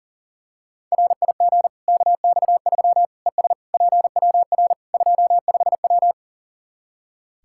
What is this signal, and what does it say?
Morse code, heard as a keyed tone, about 40 wpm, 700 Hz. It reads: RIG KX3 ES PWR 25W
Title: Practice QSO - Exchange 2 - Info - 40wpm